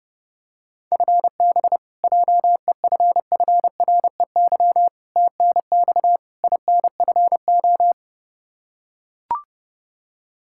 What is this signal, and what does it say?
Morse code, heard as a keyed tone, about 30 wpm, 700 Hz. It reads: FB JEFFREY TNX INFO
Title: Practice QSO - Exchange 3 - More Info - 30wpm